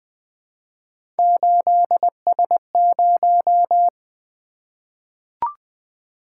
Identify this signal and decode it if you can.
Morse code, heard as a keyed tone, about 20 wpm, 700 Hz. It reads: 8S0
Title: Three Letter-Number Combinations - 20wpm